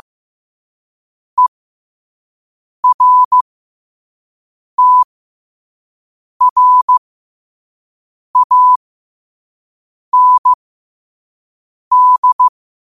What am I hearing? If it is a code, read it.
Morse code, heard as a keyed tone, about 15 wpm, spaced as for 5 wpm, 1000 Hz. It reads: ERTRAND